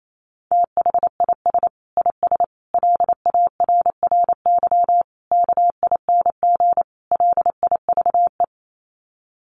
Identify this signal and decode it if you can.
Morse code, heard as a keyed tone, about 28 wpm, 700 Hz. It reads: THIS IS LARRY KING LIVE
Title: Sets of 5 Words Encore2 28wpm